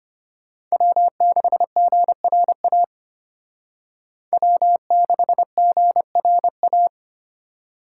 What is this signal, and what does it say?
Morse code, heard as a keyed tone, about 32 wpm, 700 Hz. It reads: W6GRA W6GRA